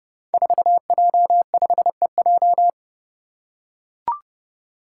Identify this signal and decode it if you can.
Morse code, heard as a keyed tone, about 30 wpm, 700 Hz. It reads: 4J5EJ
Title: Five Challenging Letter-Number Combinations 30wpm